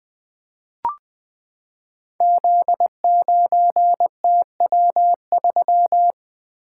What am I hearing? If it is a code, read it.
Morse code, heard as a keyed tone, about 20 wpm, 700 Hz. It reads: Z9TW3
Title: Code Groups - 5 Letter:Numbers - 20wpm